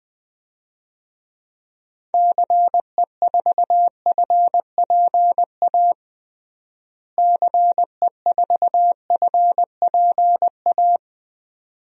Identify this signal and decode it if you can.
Morse code, heard as a keyed tone, about 20 wpm, 700 Hz. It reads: CE4FPA CE4FPA
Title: DXCC-100 Call Signs - 6 Characters - Repeated 2x - 20wpm